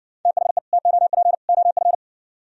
Morse code, heard as a keyed tone, about 60 wpm, 700 Hz. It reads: THE TOP OF